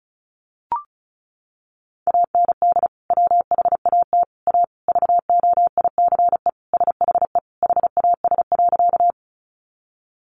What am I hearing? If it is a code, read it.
Morse code, heard as a keyed tone, about 34 wpm, 700 Hz. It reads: AND WHAT A VOICE SHE HAS.
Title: Sentences from Top 500 Words 35wpm